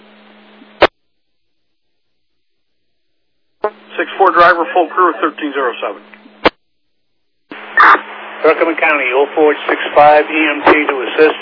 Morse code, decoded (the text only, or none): none